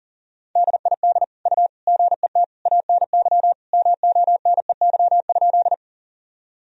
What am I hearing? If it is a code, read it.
Morse code, heard as a keyed tone, about 40 wpm, 700 Hz. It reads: DID U GET ANY MONEY?
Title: Sentences from Top 400 Words and Abbreviations 40wpm